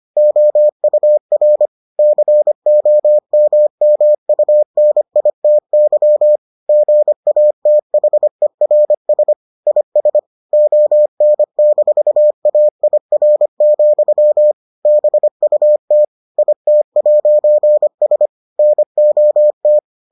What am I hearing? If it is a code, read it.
Morse code, heard as a keyed tone, about 25 wpm, 600 Hz. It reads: OUR COMMUNITY GATHERS IS ON-AIR, BUT IT'S NOT